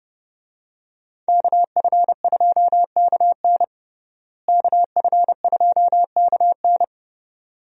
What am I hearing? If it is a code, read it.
Morse code, heard as a keyed tone, about 30 wpm, 700 Hz. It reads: KF2KN KF2KN
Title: USA Call Signs - 5 Characters - Repeated 2x - 30wpm